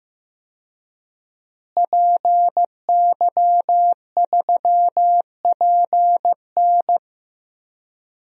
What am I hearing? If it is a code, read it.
Morse code, heard as a keyed tone, about 15 wpm, 700 Hz. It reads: PY3PN
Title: DXCC-100 Call Signs - 5 Characters - Repeated 2x - 15wpm